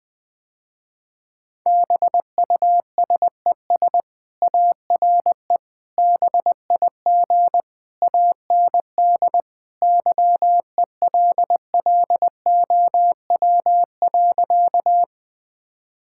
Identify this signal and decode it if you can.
Morse code, heard as a keyed tone, about 20 wpm, 700 Hz. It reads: BUSES ARE BIG AND YELLOW.
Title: Easy Sentences with Hints - 20wpm